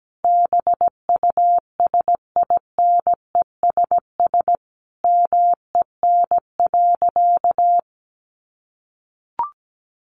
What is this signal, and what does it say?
Morse code, heard as a keyed tone, about 17 wpm, 700 Hz. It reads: BUSINESS MEN.
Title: Sentences from Top 1200 Words 17wpm